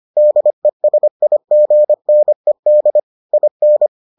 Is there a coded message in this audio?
Yes: Morse code from 0.2 to 3.9 s, heard as a keyed tone, about 25 wpm, 600 Hz. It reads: DESIGNED IN